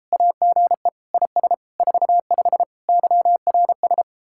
Morse code, heard as a keyed tone, about 34 wpm, 700 Hz. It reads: AGE IS 45 YRS